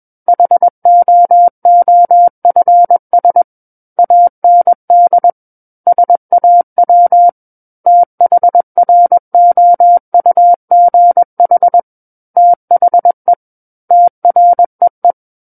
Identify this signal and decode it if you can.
Morse code, heard as a keyed tone, about 21 wpm, 700 Hz. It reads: HOOFS AND SAW THROUGH THE TREE
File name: Book One / 12-Chapter 12 - 21 WPM